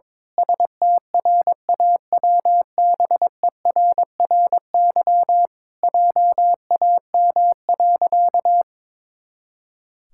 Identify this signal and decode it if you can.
Morse code, heard as a keyed tone, about 22 wpm, 700 Hz. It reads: STRAWBERRY JAM.
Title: Pangram Challenge v2 - 22wpm